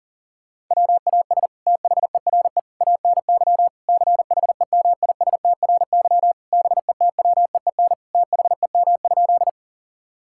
Morse code, heard as a keyed tone, about 40 wpm, 700 Hz. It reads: WAS THERE ANY CHEMISTRY BETWEEN THEM?
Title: Sentences from Top 1000 Words - Review Part 6 - 40wpm